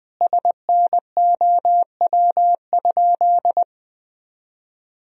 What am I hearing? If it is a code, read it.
Morse code, heard as a keyed tone, about 21 wpm, 700 Hz. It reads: SNOW?
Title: Sentences from Top 1000 Words - Review Part 3 - 20wpm